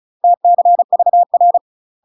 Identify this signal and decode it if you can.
Morse code, heard as a keyed tone, about 36 wpm, 700 Hz. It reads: TCVR
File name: UK MASTER QSOs TRAINING / 35WPM / UK MASTER QSOs 0016